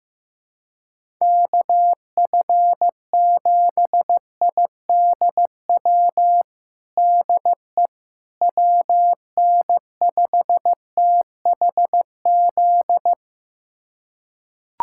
Morse code, heard as a keyed tone, about 15 wpm, 700 Hz. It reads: KF7IDW DE WN5THZ
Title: USA Callsign DE Callsign - 6 Characters - 15wpm